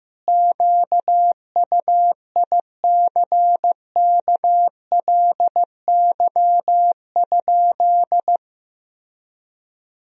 Morse code, heard as a keyed tone, about 16 wpm, 700 Hz. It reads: QUICKLY?
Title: Sentences from Top 1000 Words - Review Part 3 - 15wpm